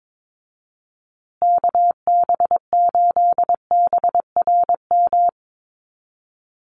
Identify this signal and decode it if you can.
Morse code, heard as a keyed tone, about 22 wpm, 700 Hz. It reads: KB8BRM
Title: USA Call Signs Repeated 2x 22wpm